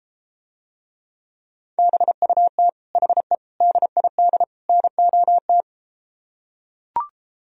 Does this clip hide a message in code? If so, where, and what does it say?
Yes: Morse code from 1.8 to 5.6 s, heard as a keyed tone, about 32 wpm, 700 Hz. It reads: BUT HE DID NOT